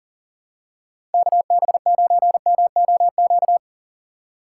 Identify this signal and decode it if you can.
Morse code, heard as a keyed tone, about 40 wpm, 700 Hz. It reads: KB9MOQ
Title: USA Call Signs - 3x Character Spacing - 40wpm